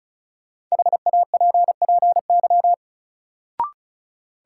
Morse code, heard as a keyed tone, about 34 wpm, 700 Hz. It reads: HAPPY
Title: Sentences from Top 1000 Words - New Word Drill - 35wpm